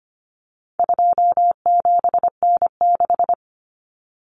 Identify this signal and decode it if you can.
Morse code, heard as a keyed tone, about 25 wpm, 700 Hz. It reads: 27N6
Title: Common Cut Numbers - 4-Digit 25wpm